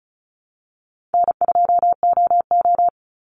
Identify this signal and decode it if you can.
Morse code, heard as a keyed tone, about 34 wpm, 700 Hz. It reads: N2OO
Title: Easy Call Signs Repeated 3x - 35wpm